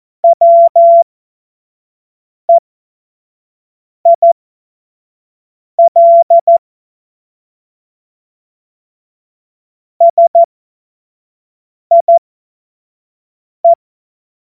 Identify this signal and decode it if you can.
Morse code, heard as a keyed tone, about 14 wpm, spaced as for 5 wpm, 680 Hz. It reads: WEIL SIE
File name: Morsekurs / Witze / Witz 16. Mai 2023 25 F70